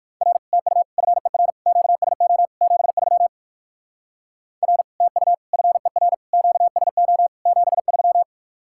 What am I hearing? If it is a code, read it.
Morse code, heard as a keyed tone, about 56 wpm, 700 Hz. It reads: R TU FER QSO 73 R TU FER QSO 73
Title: Speed-Racing Common QSO Phrases - 45wpm